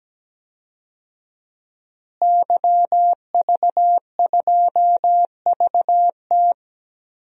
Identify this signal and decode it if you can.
Morse code, heard as a keyed tone, about 17 wpm, 700 Hz. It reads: YV2VT
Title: DXCC-100 Call Signs - 5 Characters - Repeated 3x - 17wpm